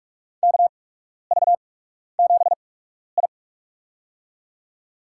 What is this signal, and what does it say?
Morse code, heard as a keyed tone, about 46 wpm, spaced as for 12 wpm, 690 Hz. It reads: KV7I
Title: USA Call Signs - 4 Characters - 8x Character Spacing - 45wpm